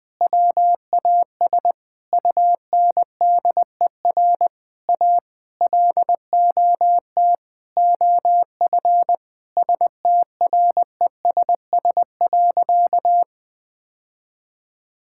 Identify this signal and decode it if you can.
Morse code, heard as a keyed tone, about 20 wpm, 700 Hz. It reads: WAS UNDER A LOT OF STRESS.
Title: Sentences from Top 1000 Words - Review Part 6 - 20wpm